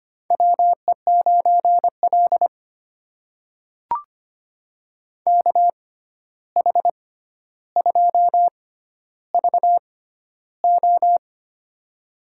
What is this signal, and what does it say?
Morse code, heard as a keyed tone, about 25 wpm, spaced as for 12 wpm, 700 Hz. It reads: WE9L KH2VO
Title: USA Call Signs - 6x Character Spacing - 25wpm